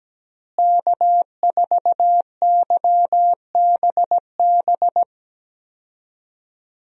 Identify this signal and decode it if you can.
Morse code, heard as a keyed tone, about 17 wpm, 700 Hz. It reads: K4YBB
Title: USA Call Signs - 5 Characters - Repeated 2x - 17wpm